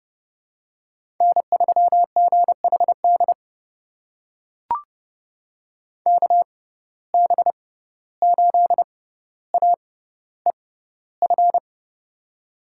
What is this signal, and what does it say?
Morse code, heard as a keyed tone, about 30 wpm, spaced as for 15 wpm, 700 Hz. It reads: N3GHD KB8AEF
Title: USA Call Signs - 6x Character Spacing - 30wpm